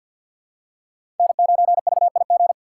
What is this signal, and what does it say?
Morse code, heard as a keyed tone, about 52 wpm, 700 Hz. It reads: N9VIG
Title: USA Call Signs - 5 Characters - 4x Character Spacing - 50wpm